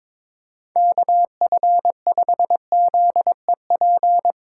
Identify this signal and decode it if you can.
Morse code, heard as a keyed tone, about 22 wpm, 700 Hz. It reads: KF5ZEP
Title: USA Call Signs - 6x Character Spacing - 22wpm